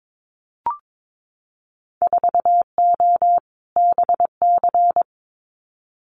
Morse code, heard as a keyed tone, about 22 wpm, 700 Hz. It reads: 4O BC
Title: Winter Field Day Exchanges Encore 22wpm